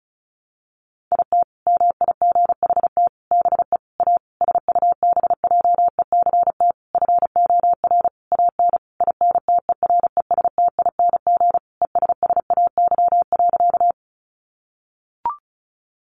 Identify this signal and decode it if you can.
Morse code, heard as a keyed tone, about 34 wpm, 700 Hz. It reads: IT MIGHT BE A SUBJECT FOR AN INTERESTING ESSAY.
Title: Sentences from Top 1500 Words 35wpm